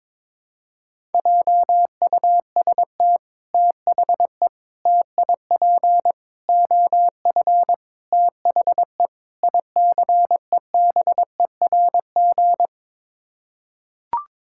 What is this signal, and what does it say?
Morse code, heard as a keyed tone, about 22 wpm, 700 Hz. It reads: JUST THE TIP OF THE ICEBERG
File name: practice / 22wpm / Sets of 6 Words - 22wpm